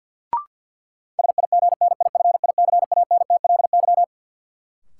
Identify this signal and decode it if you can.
Morse code, heard as a keyed tone, about 48 wpm, 700 Hz. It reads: SIGNIFICANTLY
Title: Sentences from Top 1500 Words - New Word Drill - 50wpm